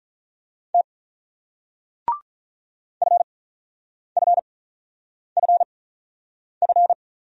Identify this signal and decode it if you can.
Morse code, heard as a keyed tone, about 47 wpm, spaced as for 9 wpm, 700 Hz. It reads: T FFFF